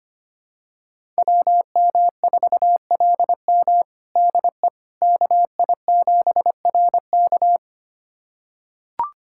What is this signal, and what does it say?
Morse code, heard as a keyed tone, about 25 wpm, 700 Hz. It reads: WM4LM DE KI7RK
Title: USA Callsign DE Callsign - 5 Characters - 25wpm